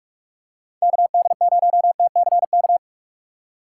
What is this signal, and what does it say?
Morse code, heard as a keyed tone, about 44 wpm, 700 Hz. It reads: KD0TCK
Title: USA Call Signs - 6 Characters - 6x Character Spacing - 45wpm